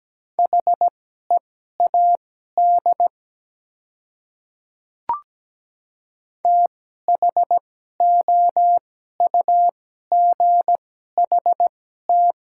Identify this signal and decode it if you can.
Morse code, heard as a keyed tone, about 17 wpm, spaced as for 12 wpm, 700 Hz. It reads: HEAD THOUGHT
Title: 2x Character Spacing - Top 300 Words - 17wpm